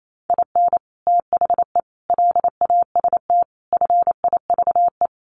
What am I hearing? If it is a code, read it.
Morse code, heard as a keyed tone, about 28 wpm, 700 Hz. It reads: IN THE LAST FIVE
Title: Sets of 5 Words Encore 28wpm